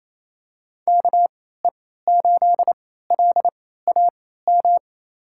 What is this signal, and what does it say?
Morse code, heard as a keyed tone, about 28 wpm, spaced as for 16 wpm, 700 Hz. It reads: KE8LAM
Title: USA Call Signs - 3x Character Spacing - 28wpm